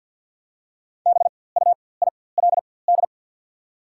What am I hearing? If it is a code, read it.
Morse code, heard as a keyed tone, about 48 wpm, spaced as for 23 wpm, 700 Hz. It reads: BUILD
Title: 4x Character Spacing - Top 300 Words - 50wpm